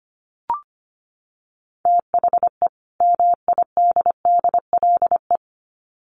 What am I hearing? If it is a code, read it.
Morse code, heard as a keyed tone, about 25 wpm, 700 Hz. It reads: THE MIDDLE
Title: Sets of 2 Words 25wpm